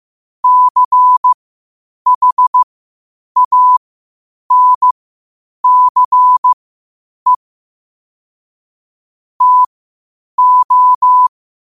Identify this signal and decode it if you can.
Morse code, heard as a keyed tone, about 15 wpm, spaced as for 8 wpm, 1000 Hz. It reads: CHANCE TO
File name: qotdfiles / 8wpm